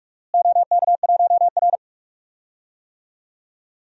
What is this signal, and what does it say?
Morse code, heard as a keyed tone, about 47 wpm, 700 Hz. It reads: OK1R
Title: DXCC-100 Call Signs - 4 Characters - Repeated 2x - 45wpm